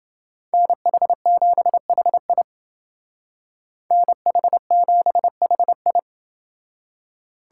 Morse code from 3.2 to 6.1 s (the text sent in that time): NH7HI